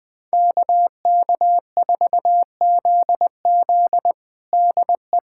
Morse code, heard as a keyed tone, about 20 wpm, 700 Hz. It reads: KK4ZZ DE
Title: USA Callsign DE Callsign - 5 Characters - 20wpm